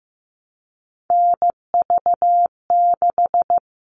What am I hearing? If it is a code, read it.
Morse code, heard as a keyed tone, about 15 wpm, 700 Hz. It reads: NV6
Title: All Cut Numbers - 3-Digit 15wpm